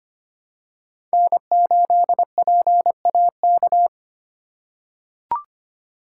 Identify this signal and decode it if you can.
Morse code, heard as a keyed tone, about 25 wpm, 700 Hz. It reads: N8PAK
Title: USA Call Signs - 5 Characters - 6x Character Spacing - 25wpm